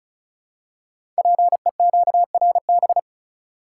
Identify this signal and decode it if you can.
Morse code, heard as a keyed tone, about 36 wpm, 690 Hz. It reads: PEQRB
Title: Code Groups - 5 Letters - 35wpm